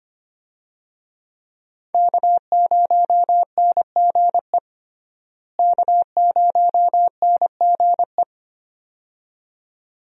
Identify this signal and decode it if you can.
Morse code, heard as a keyed tone, about 25 wpm, 700 Hz. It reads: K0NGE K0NGE
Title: USA Call Signs - 5 Characters - Repeated 2x - 25wpm